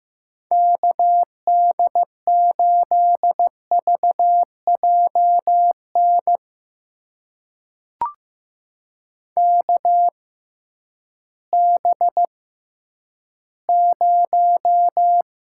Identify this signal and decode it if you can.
Morse code, heard as a keyed tone, about 15 wpm, spaced as for 10 wpm, 700 Hz. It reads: KD8VJN KB0
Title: USA Call Signs - 6 Characters - 6x Character Spacing - 15wpm